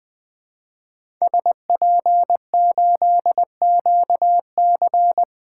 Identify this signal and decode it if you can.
Morse code, heard as a keyed tone, about 20 wpm, 700 Hz. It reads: SP8QC
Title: 2023 POTA Hunter Call Signs - 20wpm